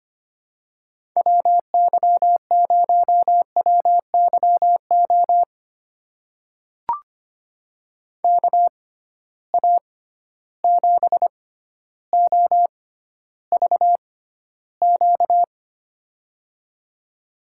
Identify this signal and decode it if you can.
Morse code, heard as a keyed tone, about 25 wpm, spaced as for 13 wpm, 700 Hz. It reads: WY0WYO KA7OVQ
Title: USA Call Signs - 6 Characters - 6x Character Spacing - 25wpm